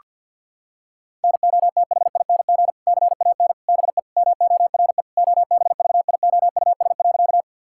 Morse code, heard as a keyed tone, about 50 wpm, 700 Hz. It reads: NOTHING CAN BE MORE OBVIOUS.
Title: Sentences from Top 1600 Words 50wpm